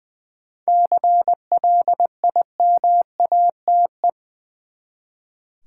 Morse code, heard as a keyed tone, about 20 wpm, 700 Hz. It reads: CLIMATE
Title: Seven Letter Words - 20wpm